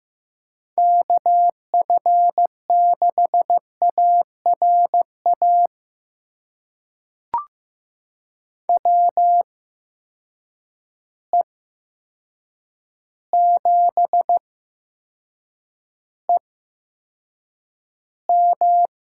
Morse code, heard as a keyed tone, about 15 wpm, spaced as for 7 wpm, 700 Hz. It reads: KF6ARA WE7EM
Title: USA Call Signs - 8x Character Spacing - 15wpm